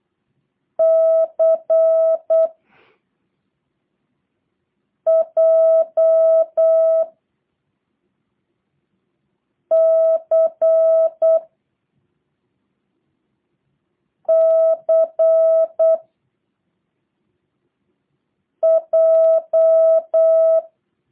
Morse code, heard as a keyed tone, about 8 wpm, 660 Hz. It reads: C J C C J
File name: jh8cbh / 2023cw / review1-5